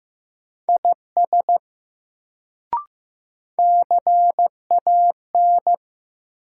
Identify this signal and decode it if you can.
Morse code, heard as a keyed tone, about 15 wpm, 700 Hz. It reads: IS CAN
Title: Sentences from Top 100 Words - New Word Drill - 15wpm